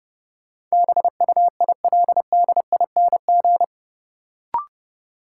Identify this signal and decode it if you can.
Morse code, heard as a keyed tone, about 31 wpm, 700 Hz. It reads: BUILDING
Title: Sentences from Top 700 Words - New Word Drill - 30wpm